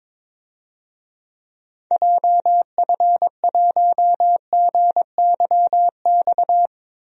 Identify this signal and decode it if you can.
Morse code, heard as a keyed tone, about 22 wpm, 700 Hz. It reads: JF1GYX
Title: DXCC-100 Call Signs - 6 Characters - Repeated 3x - 22wpm